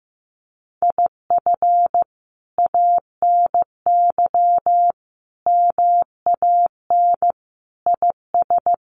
Morse code, heard as a keyed tone, about 15 wpm, 700 Hz. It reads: IF ANY MAN IS
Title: Sentences from Top 1400 Words 15wpm